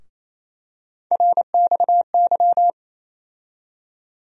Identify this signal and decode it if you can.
Morse code, heard as a keyed tone, about 28 wpm, 700 Hz. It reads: RXY
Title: USA Call Sign Postfixes - 3 Letters 28wpm